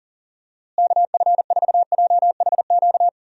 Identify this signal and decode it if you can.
Morse code, heard as a keyed tone, about 37 wpm, 700 Hz. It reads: KF4JHQ
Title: USA Call Signs - 6 Characters - Repeated 3x - 40wpm